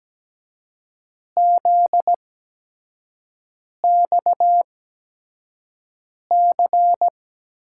Morse code, heard as a keyed tone, about 17 wpm, spaced as for 5 wpm, 700 Hz. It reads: ZXC
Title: USA Call Signs - 6 Characters - 8x Character Spacing - 17wpm